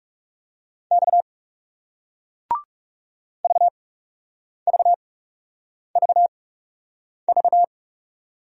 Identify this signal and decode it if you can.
Morse code, heard as a keyed tone, about 41 wpm, spaced as for 8 wpm, 700 Hz. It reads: X VVVV